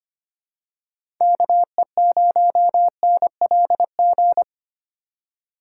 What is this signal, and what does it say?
Morse code, heard as a keyed tone, about 25 wpm, 700 Hz. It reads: KE0NLG
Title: USA Call Signs - 6 Characters - 3x Character Spacing - 25wpm